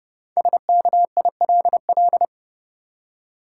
Morse code, heard as a keyed tone, about 30 wpm, 700 Hz. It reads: SKILL
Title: Five Letter Words - 30wpm